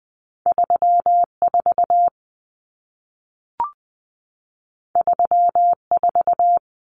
Morse code, heard as a keyed tone, about 20 wpm, 700 Hz. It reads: 34 34
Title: Common Cut Numbers - 2-Digit 20wpm